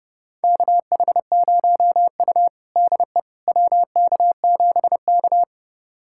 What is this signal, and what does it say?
Morse code, heard as a keyed tone, about 29 wpm, 700 Hz. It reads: KH0U DE WK7K
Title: USA Callsign DE Callsign - 4 Characters - 30wpm